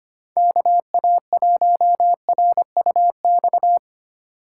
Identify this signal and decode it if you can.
Morse code, heard as a keyed tone, about 25 wpm, 700 Hz. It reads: KA1RUX
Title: USA Call Signs - 4x Character Spacing - 25wpm